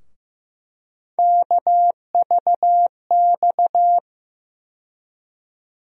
Morse code, heard as a keyed tone, about 15 wpm, 700 Hz. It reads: KVX
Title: USA Call Sign Postfixes - 3 Letters 15wpm